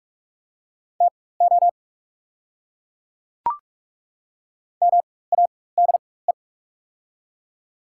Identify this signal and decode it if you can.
Morse code, heard as a keyed tone, about 44 wpm, spaced as for 21 wpm, 690 Hz. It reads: TO MADE
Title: 4x Character Spacing - Top 200 Words - 45wpm